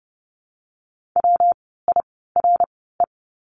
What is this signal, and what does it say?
Morse code, heard as a keyed tone, about 32 wpm, 700 Hz. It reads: W I R E
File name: course / 30:10wpm / 115 Lesson 19f - Letter w - Words 30wpm